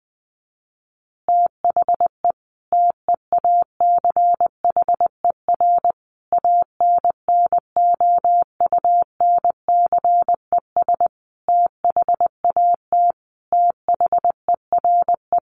Morse code, heard as a keyed tone, about 19 wpm, 700 Hz. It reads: THE TEACHER ANNOUNCES THAT THERE
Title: Additional Easy Phrases 20wpm